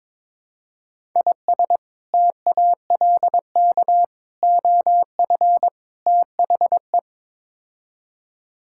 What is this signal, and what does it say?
Morse code, heard as a keyed tone, about 22 wpm, 700 Hz. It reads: IS TALK OF THE
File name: Build A Phrase - 5-Word Phrases - 22wpm